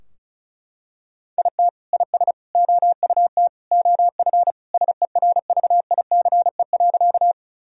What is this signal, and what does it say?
Morse code, heard as a keyed tone, about 34 wpm, 700 Hz. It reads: IT IS OUT OF SERVICE.
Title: Sentences from Top 1000 Words - Review Part 4 - 35wpm